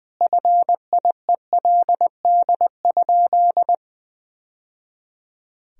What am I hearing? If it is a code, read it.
Morse code, heard as a keyed tone, about 20 wpm, 700 Hz. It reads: FIELD?
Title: Sentences from Top 1000 Words - Review Part 2 - 20wpm